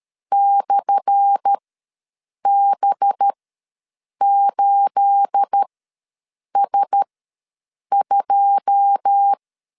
Morse code, heard as a keyed tone, about 13 wpm, spaced as for 7 wpm, 800 Hz. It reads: /B8S2
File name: LEZIONI / 10 SIGNS / 10A SLASH ws 12wpm 3443